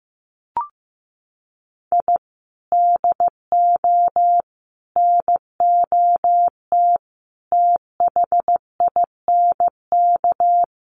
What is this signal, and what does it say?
Morse code, heard as a keyed tone, about 15 wpm, 700 Hz. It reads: I DO NOT THINK
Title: Sets of 5 Words 15wpm